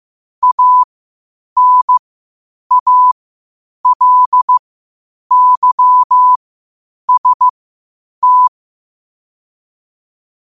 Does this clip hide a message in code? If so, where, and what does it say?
Yes: Morse code from 0.4 to 8.5 s, heard as a keyed tone, about 15 wpm, spaced as for 8 wpm, 1000 Hz. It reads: ANALYST